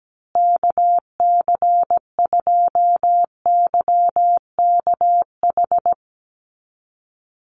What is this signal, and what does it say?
Morse code, heard as a keyed tone, about 17 wpm, 700 Hz. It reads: KC2YKH
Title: USA Call Signs Repeated 2x 17wpm